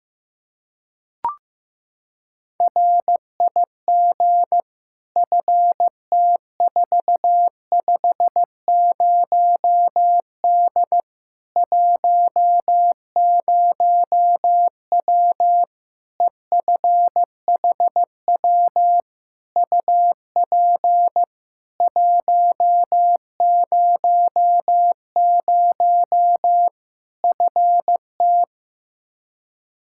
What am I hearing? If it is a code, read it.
Morse code, heard as a keyed tone, about 15 wpm, 700 Hz. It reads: RIG FT450D 10W EFHW UP 100 FT
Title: K1USN SST QSO Practice 15wpm